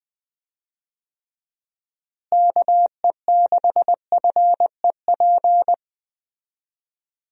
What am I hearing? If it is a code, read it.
Morse code, heard as a keyed tone, about 21 wpm, 700 Hz. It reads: KE6FEP
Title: USA Call Signs - 6 Characters - Repeated 2x - 20wpm